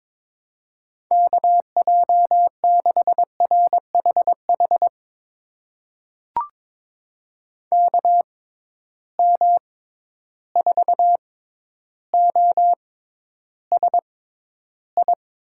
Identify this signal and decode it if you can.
Morse code, heard as a keyed tone, about 22 wpm, spaced as for 11 wpm, 700 Hz. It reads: KJ6RHH KM4OSI
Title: USA Call Signs - 6 Characters - 6x Character Spacing - 22wpm